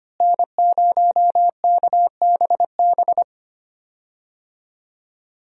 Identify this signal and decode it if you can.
Morse code, heard as a keyed tone, about 25 wpm, 700 Hz. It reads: N0KBB
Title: USA Call Signs - 5 Characters - Repeated 2x - 25wpm